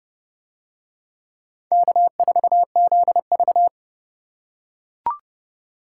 Morse code, heard as a keyed tone, about 29 wpm, 700 Hz. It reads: K4ZV